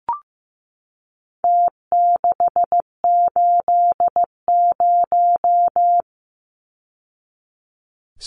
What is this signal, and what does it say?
Morse code, heard as a keyed tone, about 15 wpm, 700 Hz. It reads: T680
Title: Numbers - Contest Style - 4-Digit 15wpm